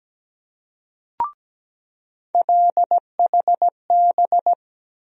Morse code, heard as a keyed tone, about 17 wpm, 700 Hz. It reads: LHB
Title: USA Call Sign Postfixes - 3 Letters 17wpm